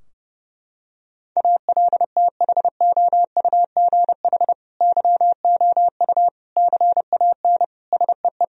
Morse code, heard as a keyed tone, about 30 wpm, 700 Hz. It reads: ALTHOUGH YOU CAN SEE
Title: Sentences from Top 1000 Words - Review Part 3 - 30wpm